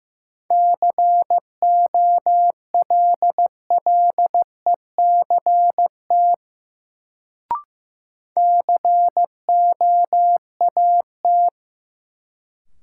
Morse code, heard as a keyed tone, about 15 wpm, 700 Hz. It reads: COLLECT COAT
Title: Sentences from Top 1600 Words - New Word Drill Alt - 15wpm